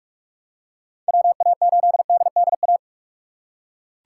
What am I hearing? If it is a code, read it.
Morse code, heard as a keyed tone, about 43 wpm, 700 Hz. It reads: WA8DDA
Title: USA Call Signs - 6 Characters - 2x Character Spacing - 45wpm